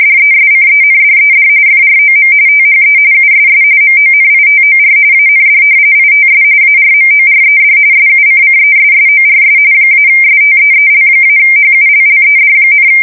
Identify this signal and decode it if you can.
RTTY, about 45.45 baud. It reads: DESCENDED FROM THE USS MONITOR AND ITS ARMORED SUCCESSORS -STEEL-HULLED,
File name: fsk170 / battleshipnj